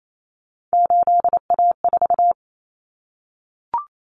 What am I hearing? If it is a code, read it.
Morse code, heard as a keyed tone, about 28 wpm, 700 Hz. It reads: 8A4
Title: Common Cut Numbers - 3-Digit 28wpm